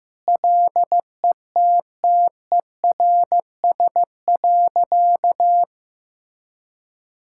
Morse code, heard as a keyed tone, about 15 wpm, 700 Hz. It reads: LETTERS.
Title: Sentences from Top 1000 Words - Review Part 6 - 15wpm